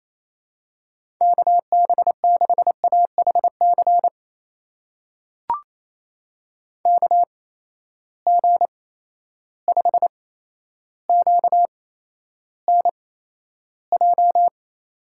Morse code, heard as a keyed tone, about 28 wpm, spaced as for 12 wpm, 700 Hz. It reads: KB6AHC KG5QNJ
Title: USA Call Signs - 6 Characters - 8x Character Spacing - 28wpm